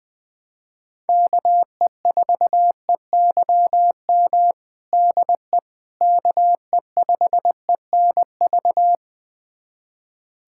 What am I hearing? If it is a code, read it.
Morse code, heard as a keyed tone, about 20 wpm, 700 Hz. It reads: KE4EYM DE KE5ENV
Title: USA Callsign DE Callsign - 6 Characters - 20wpm